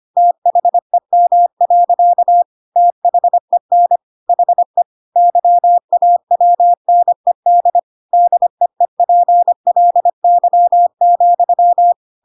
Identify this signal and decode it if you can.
Morse code, heard as a keyed tone, about 25 wpm, 700 Hz. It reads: THEM. THEN HE YAWNED DEEPLY,
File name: Pinocchio / Pinocchio ch7 25 25 ik1hge